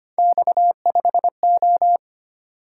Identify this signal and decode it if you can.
Morse code, heard as a keyed tone, about 29 wpm, 700 Hz. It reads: X5O
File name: Speed-Racing - Three Letter-Number Combinations - 25wpm